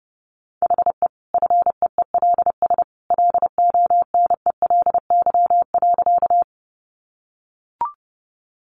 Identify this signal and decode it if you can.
Morse code, heard as a keyed tone, about 30 wpm, 700 Hz. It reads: HE FEELS LONELY.
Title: Easy Phrases 30wpm